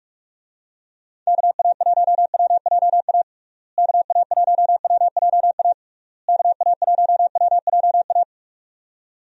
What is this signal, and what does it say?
Morse code, heard as a keyed tone, about 46 wpm, 700 Hz. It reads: KA1WJA KA1WJA KA1WJA
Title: USA Call Signs - 6 Characters - Repeated 3x - 45wpm